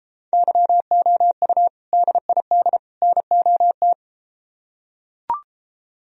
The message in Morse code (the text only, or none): YOU DID NOT